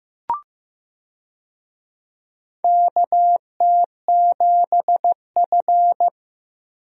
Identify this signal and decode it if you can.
Morse code, heard as a keyed tone, about 15 wpm, 700 Hz. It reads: KT7F
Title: USA Call Signs - 4 Characters - Repeated 2x - 15wpm